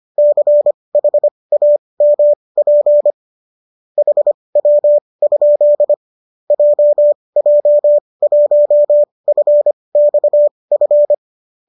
Morse code, heard as a keyed tone, about 25 wpm, spaced as for 20 wpm, 590 Hz. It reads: CHAMP HW? JJ1FXF
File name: qso208 20